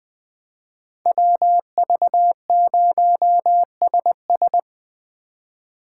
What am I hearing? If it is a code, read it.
Morse code, heard as a keyed tone, about 23 wpm, 700 Hz. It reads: WV0SS
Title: Speed Racing - USA 2x2 Call Signs 20wpm